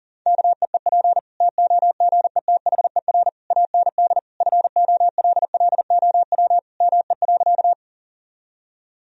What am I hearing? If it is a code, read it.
Morse code, heard as a keyed tone, about 40 wpm, 700 Hz. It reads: KEEP TOGETHER AND FOLLOW ME.
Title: Sentences from Top 1000 Words - Review Part 1 - 40wpm